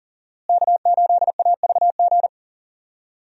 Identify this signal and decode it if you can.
Morse code, heard as a keyed tone, about 40 wpm, 700 Hz. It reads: K8AVG
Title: USA Call Signs - 5 Characters - Repeated 3x - 40wpm